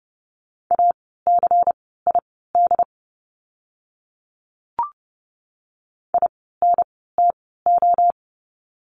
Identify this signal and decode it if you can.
Morse code, heard as a keyed tone, about 30 wpm, spaced as for 17 wpm, 700 Hz. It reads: ACID INTO
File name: Lesson 16f - Letter c - Words 30wpm